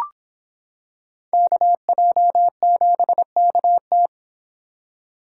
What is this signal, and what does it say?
Morse code, heard as a keyed tone, about 26 wpm, 700 Hz. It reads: KJ7KT